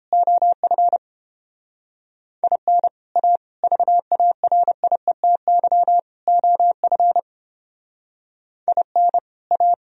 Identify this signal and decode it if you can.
Morse code, heard as a keyed tone, about 29 wpm, 700 Hz. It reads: OF IN A VARIETY OF IN A